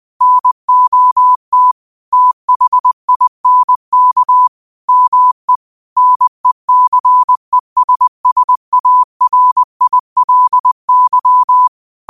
Morse code, heard as a keyed tone, about 20 wpm, 1000 Hz. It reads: NOT THINK ME NECESSARILY
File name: qotdfiles / 20wpm